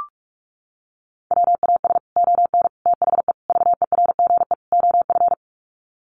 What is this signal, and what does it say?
Morse code, heard as a keyed tone, about 44 wpm, 700 Hz. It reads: WAS ON THE VERGE OF
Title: Sets of 5 Words Encore 45wpm